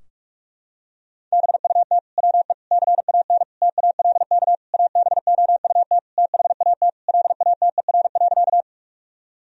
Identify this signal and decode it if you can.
Morse code, heard as a keyed tone, about 41 wpm, 700 Hz. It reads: BUT WE CAN TALK ABOUT THAT LATER.
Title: Sentences from Top 1000 Words - Review Part 4 - 45wpm